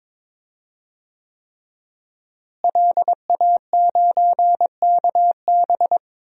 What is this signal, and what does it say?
Morse code, heard as a keyed tone, about 22 wpm, 700 Hz. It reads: LA9KB
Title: DXCC-100 Call Signs - 5 Characters - Repeated 3x - 22wpm